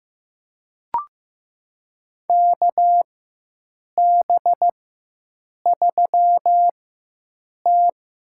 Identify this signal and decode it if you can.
Morse code, heard as a keyed tone, about 15 wpm, spaced as for 7 wpm, 700 Hz. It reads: KB3T
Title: USA Call Signs - 6 Characters - 4x Character Spacing - 15wpm